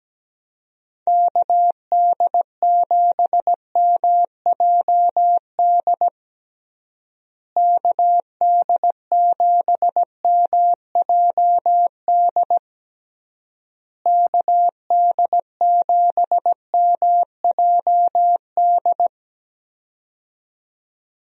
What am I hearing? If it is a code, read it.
Morse code, heard as a keyed tone, about 17 wpm, 700 Hz. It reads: KD7MJD KD7MJD KD7MJD
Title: USA Call Signs - 6 Characters - Repeated 3x - 17wpm